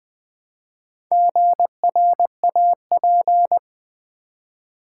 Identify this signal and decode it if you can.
Morse code, heard as a keyed tone, about 20 wpm, 700 Hz. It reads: GRAP
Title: Build A Word - 2 Letters At A Time - 8-Letter Words - 20wpm